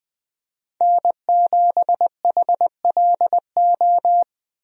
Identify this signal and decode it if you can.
Morse code, heard as a keyed tone, about 21 wpm, 700 Hz. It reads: N7HLO
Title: USA Call Signs - 5 Characters - Repeated 2x - 20wpm